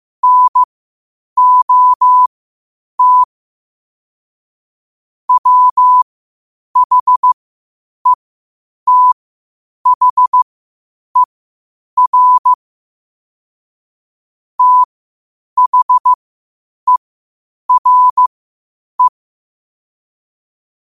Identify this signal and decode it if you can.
Morse code, heard as a keyed tone, about 15 wpm, spaced as for 8 wpm, 1000 Hz. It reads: NOT WHETHER THERE
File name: qotdfiles / 8wpm